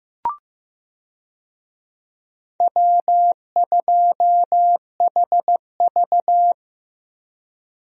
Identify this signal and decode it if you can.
Morse code, heard as a keyed tone, about 15 wpm, 700 Hz. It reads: W2HV
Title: USA Call Signs - 4 Characters - Repeated 3x - 15wpm